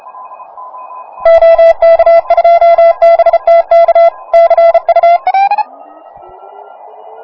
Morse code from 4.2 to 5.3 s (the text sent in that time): CU